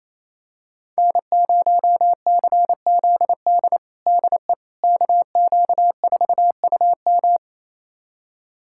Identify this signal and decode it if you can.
Morse code, heard as a keyed tone, about 28 wpm, 700 Hz. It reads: N0CZD DE KQ4UM
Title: USA Callsign DE Callsign - 5 Characters - 28wpm